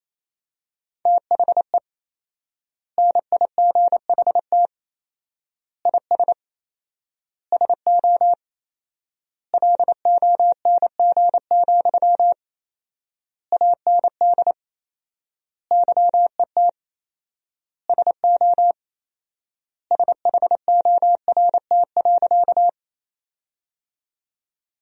Morse code, heard as a keyed tone, about 28 wpm, 700 Hz. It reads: THE NIGHT IS SO LONG, AND YET SO SHORT.
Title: 4x Word-Spacing - Sentences from Top 500 Words - 28wpm